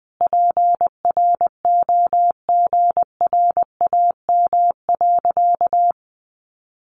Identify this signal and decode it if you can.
Morse code, heard as a keyed tone, about 20 wpm, 700 Hz. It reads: PROGRAM.
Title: Sentences from Top 1500 Words 20wpm